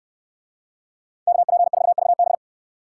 Morse code, heard as a keyed tone, about 62 wpm, 700 Hz. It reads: KP4VB